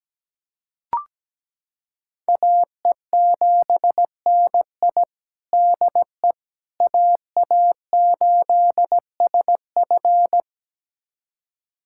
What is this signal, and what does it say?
Morse code, heard as a keyed tone, about 17 wpm, 700 Hz. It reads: AE7NI DE AA8SF
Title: USA Callsign DE Callsign - 5 Characters - 17wpm